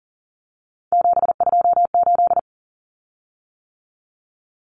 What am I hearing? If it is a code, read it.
Morse code, heard as a keyed tone, about 40 wpm, 700 Hz. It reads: 728
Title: Common Cut Numbers - 3-Digit 40wpm